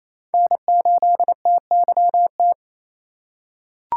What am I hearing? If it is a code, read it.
Morse code, heard as a keyed tone, about 28 wpm, 700 Hz. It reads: N8TYT
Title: USA Call Signs - 5 Characters - 4x Character Spacing - 28wpm